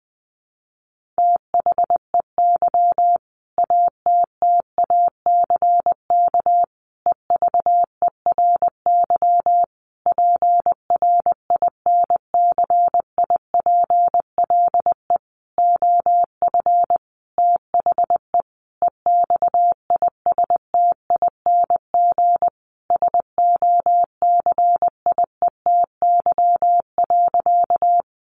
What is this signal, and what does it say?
Morse code, heard as a keyed tone, about 20 wpm, 700 Hz. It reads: THEY ATTACK EVERY PRINCIPLE OF THE EXISTING SOCIETY.
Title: Sentences from Top 1500 Words 20wpm